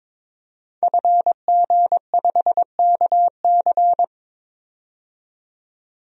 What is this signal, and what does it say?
Morse code, heard as a keyed tone, about 21 wpm, 700 Hz. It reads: FG5KC
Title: DXCC-100 Call Signs - 5 Characters - 22wpm